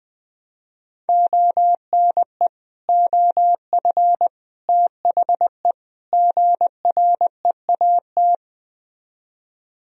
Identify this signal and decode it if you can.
Morse code, heard as a keyed tone, about 20 wpm, 700 Hz. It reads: ONE OF THE GREAT